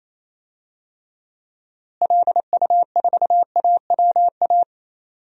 Morse code, heard as a keyed tone, about 28 wpm, 700 Hz. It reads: LU4AWA
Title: DXCC-100 Call Signs - 6 Characters - Repeated 3x - 28wpm